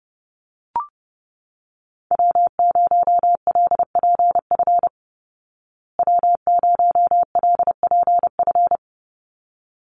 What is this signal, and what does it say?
Morse code, heard as a keyed tone, about 30 wpm, 700 Hz. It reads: W0LPF W0LPF
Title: Call Signs Repeated 3x - 30wpm